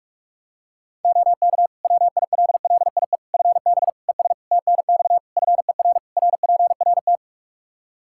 Morse code, heard as a keyed tone, about 46 wpm, 700 Hz. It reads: OK WILLIE FB ES TNX FER RPRT
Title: Practice QSO - Exchange 2 - Info - 45wpm